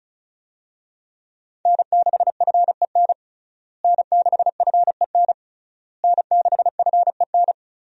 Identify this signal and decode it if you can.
Morse code, heard as a keyed tone, about 36 wpm, 700 Hz. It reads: N6FEN N6FEN N6FEN
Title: USA Call Signs - 5 Characters - Repeated 3x - 35wpm